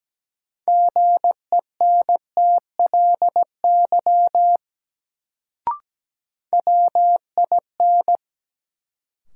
Morse code, heard as a keyed tone, about 17 wpm, 700 Hz. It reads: GENTLY WIN
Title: Sentences from Top 1500 Words - New Word Drill Alt - 17wpm